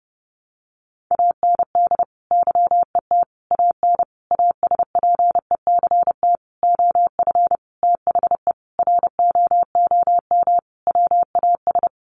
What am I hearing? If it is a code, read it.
Morse code, heard as a keyed tone, about 30 wpm, 700 Hz. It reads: AND YET AN ASPECT OF THE ROOM WAS
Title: Sentences from Top 1300 Words 30wpm